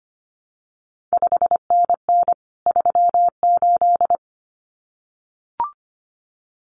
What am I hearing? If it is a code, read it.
Morse code, heard as a keyed tone, about 25 wpm, 700 Hz. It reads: 5NN 38
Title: CQ WW DX Contest Exchange Encore - 25wpm